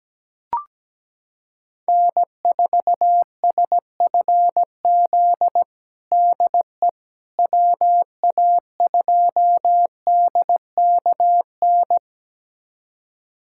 Morse code, heard as a keyed tone, about 17 wpm, 700 Hz. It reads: N4SFZ DE WA2DKN
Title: USA Callsign DE Callsign 17wpm